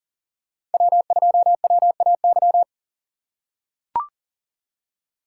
Morse code, heard as a keyed tone, about 37 wpm, 700 Hz. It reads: W2WAY
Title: USA Call Signs - 5 Characters - 4x Character Spacing - 40wpm